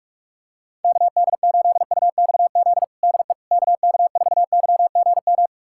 Morse code, heard as a keyed tone, about 41 wpm, 700 Hz. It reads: KD8UXZ DE KK4YGM
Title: USA Callsign DE Callsign - 6 Characters - 45wpm